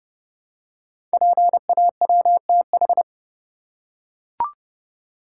Morse code, heard as a keyed tone, about 30 wpm, 700 Hz. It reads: PAWTH
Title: Code Groups - 5 Letters - 30wpm